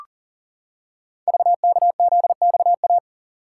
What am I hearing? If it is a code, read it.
Morse code, heard as a keyed tone, about 38 wpm, 690 Hz. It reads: VKZXA
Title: Code Groups - 5 Letters - 40wpm